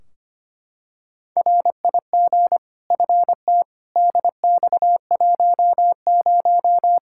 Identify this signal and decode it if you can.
Morse code, heard as a keyed tone, about 25 wpm, 700 Hz. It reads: RIG FT DX10
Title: K1USN SST QSO Practice 25wpm